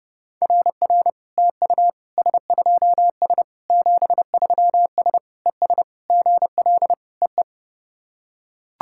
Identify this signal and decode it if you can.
Morse code, heard as a keyed tone, about 30 wpm, 700 Hz. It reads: RR TU S2S 73S ES GL EE
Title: SOTA QSO Practice 30wpm